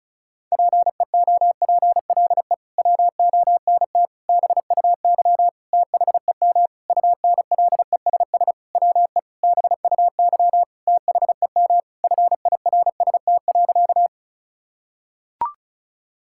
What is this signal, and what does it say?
Morse code, heard as a keyed tone, about 36 wpm, 700 Hz. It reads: PEOPLE WONT BUY THEM UNLESS WE BUY THEM FIRST.
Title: Sentences from Top 1000 Words - Review Part 6 - 35wpm